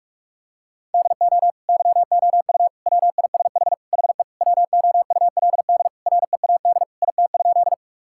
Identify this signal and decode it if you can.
Morse code, heard as a keyed tone, about 47 wpm, 700 Hz. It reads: DO YOU WISH HE WOULD READ IT?
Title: Sentences from Top 1600 Words 45wpm